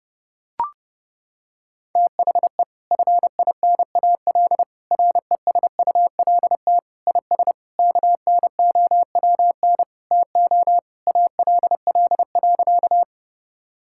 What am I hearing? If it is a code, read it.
Morse code, heard as a keyed tone, about 28 wpm, 700 Hz. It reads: THE FINAL RESULT IS KNOWN TO ALL.
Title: Sentences from Top 1000 Words - Review Part 5 - 30wpm